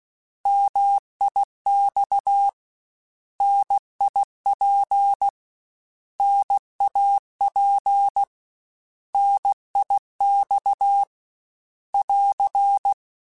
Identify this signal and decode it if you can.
Morse code, heard as a keyed tone, about 16 wpm, 800 Hz. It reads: MIX NIP NAP NIX <AR>
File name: Contrasts / Contrasts-mp3s / 3LtrContrasts @15